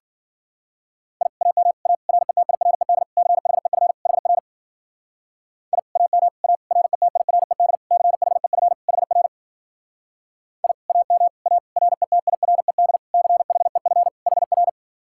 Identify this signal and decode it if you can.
Morse code, heard as a keyed tone, about 51 wpm, 700 Hz. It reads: I AM A RETIRED CHEF HR I AM A RETIRED CHEF HR I AM A RETIRED CHEF HR